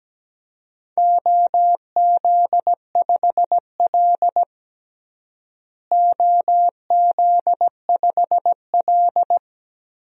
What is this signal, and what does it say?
Morse code, heard as a keyed tone, about 17 wpm, 700 Hz. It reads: OZ5L OZ5L
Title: DXCC-100 Call Signs - 4 Characters - Repeated 3x - 17wpm